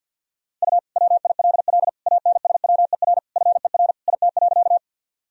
Morse code, heard as a keyed tone, about 50 wpm, 700 Hz. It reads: U WILL ANSWER FER IT.
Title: Sentences from Top 300 Words and Abbreviations 50wpm